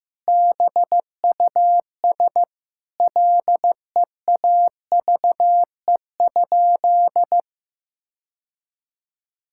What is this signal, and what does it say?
Morse code, heard as a keyed tone, about 15 wpm, 700 Hz. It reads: BUS LEAVE?
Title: Sentences from Top 1000 Words - Review Part 6 - 15wpm